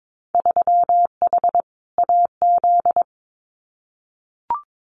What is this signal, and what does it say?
Morse code, heard as a keyed tone, about 22 wpm, 700 Hz. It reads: 3H AZ
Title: Winter Field Day Exchanges Encore 22wpm